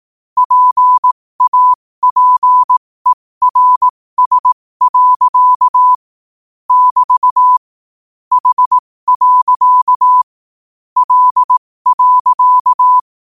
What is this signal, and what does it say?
Morse code, heard as a keyed tone, about 18 wpm, spaced as for 15 wpm, 1000 Hz. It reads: PAPERS. <BT> H. L.
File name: qotdfiles / 15wpm